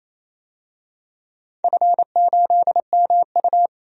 Morse code, heard as a keyed tone, about 28 wpm, 700 Hz. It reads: F8MU
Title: DXCC-100 Call Signs - 4 Characters - 28wpm